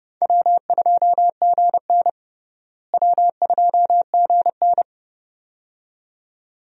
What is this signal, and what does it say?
Morse code, heard as a keyed tone, about 30 wpm, 700 Hz. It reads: W2GN W2GN
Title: USA Call Signs - 4 Characters - Repeated 2x - 30wpm